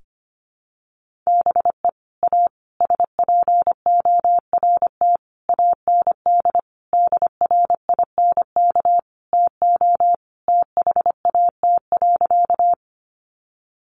Morse code, heard as a keyed tone, about 25 wpm, 700 Hz. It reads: BE A SPORT AND DRINK TO THAT.
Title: Sentences from Top 1500 Words 25wpm